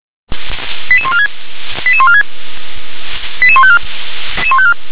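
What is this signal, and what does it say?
Five-tone selective call (ZVEI), heard as quick runs of five tones, tones of about 70 ms. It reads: 88146 88146 88154
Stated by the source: ZVEI five-tone selective call 88641 received on SDR.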